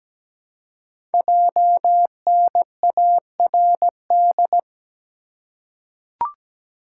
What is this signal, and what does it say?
Morse code, heard as a keyed tone, about 17 wpm, 700 Hz. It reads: JNARD
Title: Code Groups - 5 Letters - 17wpm